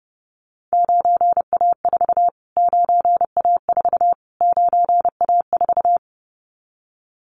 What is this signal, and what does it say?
Morse code, heard as a keyed tone, about 30 wpm, 690 Hz. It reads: 9A4 9A4 9A4
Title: Callsign Prefixes for Top 25 DXCC - Repeated 3x - 30wpm